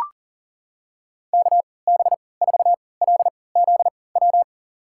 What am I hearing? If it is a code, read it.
Morse code, heard as a keyed tone, about 40 wpm, spaced as for 23 wpm, 700 Hz. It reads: KB4LZW
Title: USA Call Signs - 6 Characters - 3x Character Spacing - 40wpm